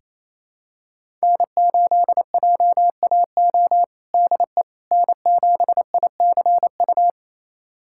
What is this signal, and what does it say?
Morse code, heard as a keyed tone, about 28 wpm, 700 Hz. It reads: N8JAO DE N7ICU
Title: USA Callsign DE Callsign 28wpm